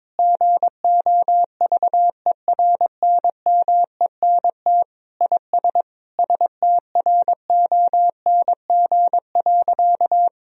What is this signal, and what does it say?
Morse code, heard as a keyed tone, about 22 wpm, 700 Hz. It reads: GOVERNMENT IS STRONG.
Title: Sentences from Top 1000 Words - Review Part 4 - 22wpm